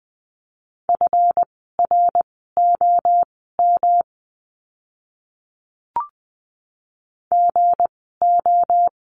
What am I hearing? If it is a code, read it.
Morse code, heard as a keyed tone, about 20 wpm, spaced as for 15 wpm, 700 Hz. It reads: FROM GO
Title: 237 Lesson 38f - Letter q - Words 20wpm